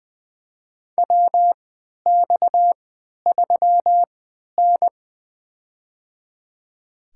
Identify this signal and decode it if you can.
Morse code, heard as a keyed tone, about 20 wpm, spaced as for 11 wpm, 700 Hz. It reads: WX3N
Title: USA Call Signs - 3x Character Spacing - 20wpm